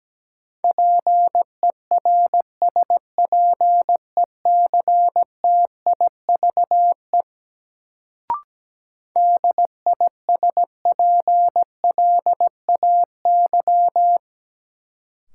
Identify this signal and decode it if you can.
Morse code, heard as a keyed tone, about 17 wpm, 700 Hz. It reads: PERSPECTIVE DISPLAY
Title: Sentences from Top 1200 Words - New Word Drill - 17wpm